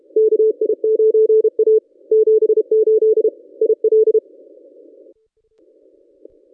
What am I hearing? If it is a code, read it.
Morse code, heard as a keyed tone, about 28 wpm, 440 Hz. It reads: KI9A 78 IL
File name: contests / ss-cw-k5tr-op-n5rz -1000